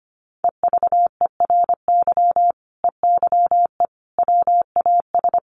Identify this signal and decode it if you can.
Morse code, heard as a keyed tone, about 25 wpm, 700 Hz. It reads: EVERY EYE WAS
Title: Sentences from Top 1100 Words 25wpm